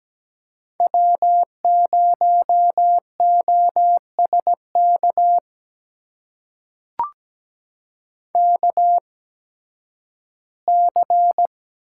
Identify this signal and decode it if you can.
Morse code, heard as a keyed tone, about 17 wpm, spaced as for 11 wpm, 700 Hz. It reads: W0OSK KC
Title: USA Call Signs - 5 Characters - 8x Character Spacing - 17wpm